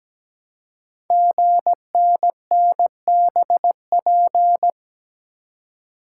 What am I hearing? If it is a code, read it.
Morse code, heard as a keyed tone, about 17 wpm, 700 Hz. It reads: GNNBP
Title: Code Groups - 5 Letters - 17wpm